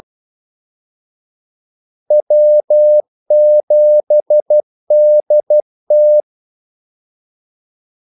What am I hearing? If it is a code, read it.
Morse code, heard as a keyed tone, about 12 wpm, 600 Hz. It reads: W7DT
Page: 2 ekstra ordmellemrum mellem kaldesignalerne